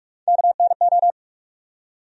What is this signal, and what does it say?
Morse code, heard as a keyed tone, about 46 wpm, 700 Hz. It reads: KNO
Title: USA Call Sign Postfixes - 3 Letters 45wpm